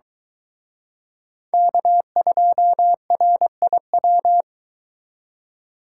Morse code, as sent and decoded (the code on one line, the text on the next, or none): -.- ..--- .-. .. .--
K2RIW